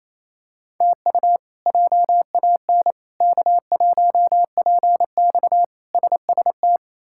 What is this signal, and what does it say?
Morse code, heard as a keyed tone, about 28 wpm, 700 Hz. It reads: TU JAN K1PX SST
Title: K1USN SST QSO Practice - 28wpm